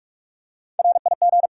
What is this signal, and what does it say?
Morse code, heard as a keyed tone, about 44 wpm, 700 Hz. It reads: RIG
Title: Practice QSO - Exchange 2 - Info - 45wpm